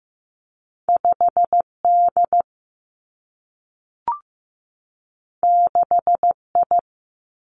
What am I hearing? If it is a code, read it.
Morse code, heard as a keyed tone, about 15 wpm, 700 Hz. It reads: 5 D 6 I
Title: Two Challenging Letter-Number Combinations 15wpm